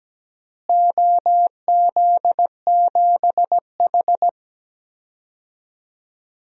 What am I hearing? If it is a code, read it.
Morse code, heard as a keyed tone, about 17 wpm, 700 Hz. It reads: OZ7H
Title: DXCC-100 Call Signs - 4 Characters - Repeated 2x - 17wpm